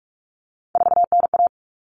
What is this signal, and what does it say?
Morse code, heard as a keyed tone, about 44 wpm, 700 Hz. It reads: 4NA
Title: Common Cut Numbers - 3-Digit 45wpm